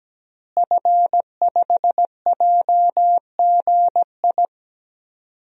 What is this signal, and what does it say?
Morse code, heard as a keyed tone, about 17 wpm, 700 Hz. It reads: F5JGI
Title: DXCC-100 Call Signs - 5 Characters - Repeated 2x - 17wpm